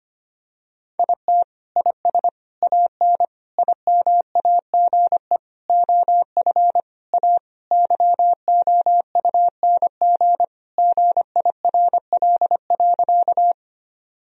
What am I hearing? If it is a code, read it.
Morse code, heard as a keyed tone, about 25 wpm, 700 Hz. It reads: IT IS AN IMAGE OF A YOUNG GIRL.
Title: Sentences from Top 1000 Words - Review Part 4 - 25wpm